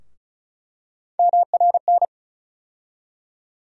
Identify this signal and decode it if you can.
Morse code, heard as a keyed tone, about 36 wpm, 700 Hz. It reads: MRN
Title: USA Call Sign Postfixes - 3 Letters 35wpm